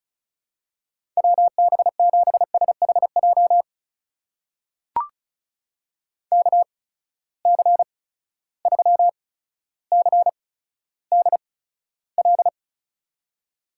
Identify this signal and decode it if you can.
Morse code, heard as a keyed tone, about 36 wpm, spaced as for 15 wpm, 700 Hz. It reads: WB7SHJ KC3CDL
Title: USA Call Signs - 6 Characters - 8x Character Spacing - 35wpm